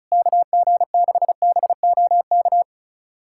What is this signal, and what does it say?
Morse code, heard as a keyed tone, about 34 wpm, 700 Hz. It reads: KG6BOK
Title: USA Callsign DE Callsign - 6 Characters - 35wpm